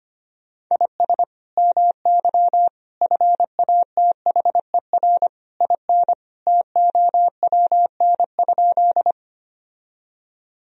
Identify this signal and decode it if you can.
Morse code, heard as a keyed tone, about 25 wpm, 700 Hz. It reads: IS MY FATHER IN TOWN?
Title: Sentences from Top 1000 Words - Review Part 2 - 25wpm